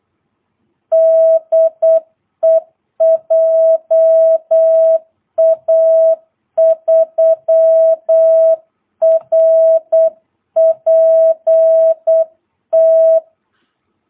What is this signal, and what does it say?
Morse code, heard as a keyed tone, about 8 wpm, 660 Hz. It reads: DE JA3RPT
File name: jh8cbh / 2022cw